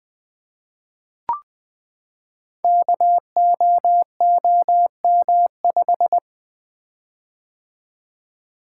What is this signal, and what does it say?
Morse code, heard as a keyed tone, about 20 wpm, 700 Hz. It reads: KOOM5
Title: Code Groups - 5 Letter:Numbers - 20wpm